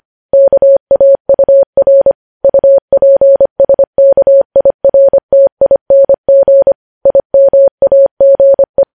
Morse code, heard as a keyed tone, about 25 wpm, 550 Hz. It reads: KAUR UPSKIRTING IMAGE
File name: podcast / News headlines-1x-25x25